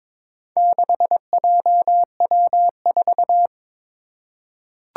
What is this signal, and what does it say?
Morse code, heard as a keyed tone, about 21 wpm, 700 Hz. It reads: 6JW4
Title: Four Challenging Letter-Number Combinations 22wpm